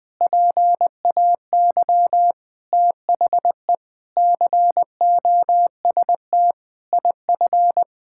Morse code, heard as a keyed tone, about 21 wpm, 700 Hz. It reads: PAY THE COST IF
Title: Sets of 8 Words - 20wpm